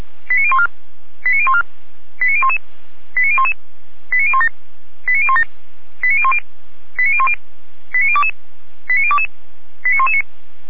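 Five-tone selective call (ZVEI), heard as quick runs of five tones, tones of about 70 ms. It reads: 89014 89014 89011 89011 89017 89017 89010 89010 89022 89022 89119
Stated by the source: Five-tone ZVEI selective call received by SDR.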